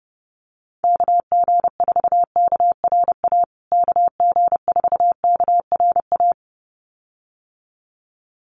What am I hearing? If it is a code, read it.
Morse code, heard as a keyed tone, about 30 wpm, 700 Hz. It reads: KG4KRA KG4KRA
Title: USA Call Signs Repeated 2x 30wpm